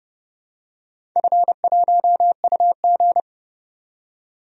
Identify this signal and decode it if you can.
Morse code, heard as a keyed tone, about 30 wpm, 700 Hz. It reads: F1UG
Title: DXCC-100 Call Signs - 4 Characters - 30wpm